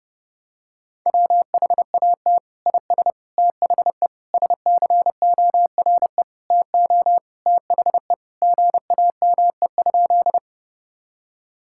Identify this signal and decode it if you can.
Morse code, heard as a keyed tone, about 32 wpm, 700 Hz. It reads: WHAT IS THE SCORE TO THE GAME?
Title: Sentences from Top 1600 Words 30wpm